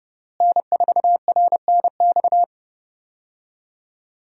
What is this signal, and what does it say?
Morse code, heard as a keyed tone, about 30 wpm, 700 Hz. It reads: N4RNX
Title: USA Call Signs - 5 Characters - Repeated 2x - 30wpm